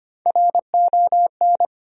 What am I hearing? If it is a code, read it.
Morse code, heard as a keyed tone, about 25 wpm, 700 Hz. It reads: RON
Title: USA Common Names - Max 4 Letters - 25wpm